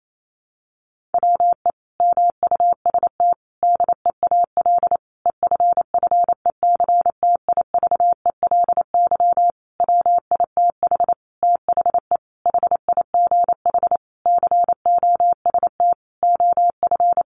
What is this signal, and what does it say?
Morse code, heard as a keyed tone, about 28 wpm, 700 Hz. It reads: WE MUST DEAL EFFECTIVELY WITH THE HIGH COST OF
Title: Sentences from Top 1400 Words 28wpm